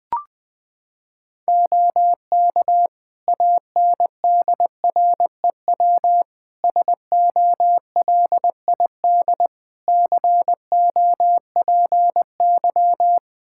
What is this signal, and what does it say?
Morse code, heard as a keyed tone, about 20 wpm, 700 Hz. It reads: OK ANDREW SOLID COPY
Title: Practice QSO - Exchange 3 - More Info - 20wpm